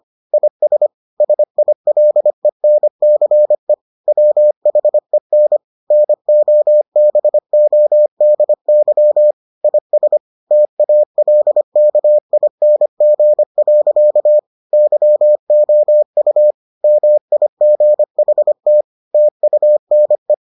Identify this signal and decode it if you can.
Morse code, heard as a keyed tone, about 25 wpm, 600 Hz. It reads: IS SILENCE WHEN NOBODY IS TALKING. YOU MIGHT TUNE